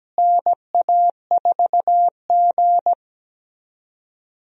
Morse code, heard as a keyed tone, about 17 wpm, 700 Hz. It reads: NA4G
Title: USA Call Signs - 4 Characters - Repeated 3x - 17wpm